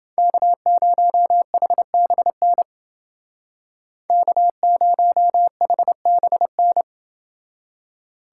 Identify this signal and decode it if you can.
Morse code, heard as a keyed tone, about 28 wpm, 700 Hz. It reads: K0HBN K0HBN